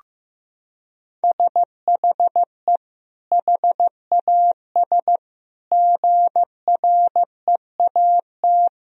Morse code, heard as a keyed tone, about 15 wpm, 700 Hz. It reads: SHE HAS GREAT
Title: Sentences from Top 1000 Words - Review Part 4 - 15wpm